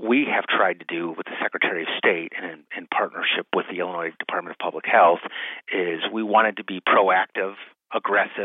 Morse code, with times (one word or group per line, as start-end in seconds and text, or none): none